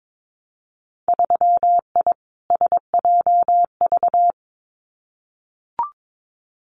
Morse code, heard as a keyed tone, about 22 wpm, 700 Hz. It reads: 3I SJV
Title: Winter Field Day Exchanges Encore 22wpm